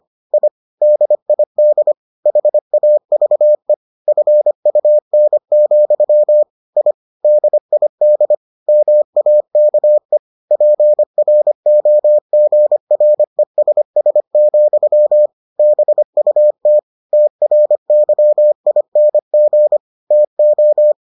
Morse code, heard as a keyed tone, about 25 wpm, 600 Hz. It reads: I DID HAVE FUN, I DID MAKE PROGRESS, BUT TRYING TO